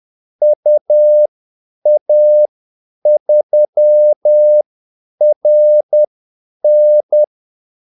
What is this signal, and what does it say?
Morse code, heard as a keyed tone, about 10 wpm, spaced as for 8 wpm, 600 Hz. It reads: UA3RN
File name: Callsigns-10-8wpm